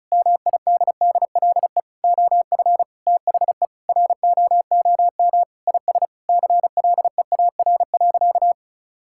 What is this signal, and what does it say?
Morse code, heard as a keyed tone, about 36 wpm, 700 Hz. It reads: MIDDLE OF THE ROOM IS CLEAR.
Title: Sentences from Top 1000 Words - Review Part 6 - 35wpm